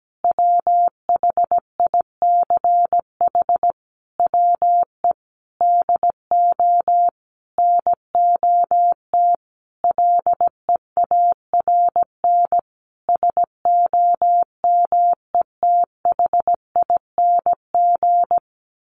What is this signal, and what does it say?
Morse code, heard as a keyed tone, about 17 wpm, 700 Hz. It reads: WHICH WE DO NOT LEARN SOMETHING
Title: Sentences from Top 1400 Words 17wpm